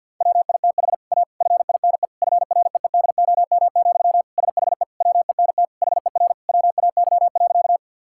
Morse code, heard as a keyed tone, about 50 wpm, 700 Hz. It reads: WITH A FINE FREEDOM, SHE WENT HER WAY.
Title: Sentences from Top 1000 Words - Review Part 5 - 50wpm